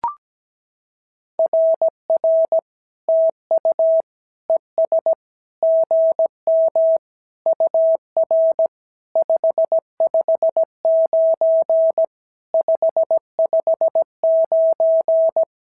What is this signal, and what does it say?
Morse code, heard as a keyed tone, about 17 wpm, 650 Hz. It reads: RR TU ES GM UR 559 559
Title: SOTA QSO Practice 17wpm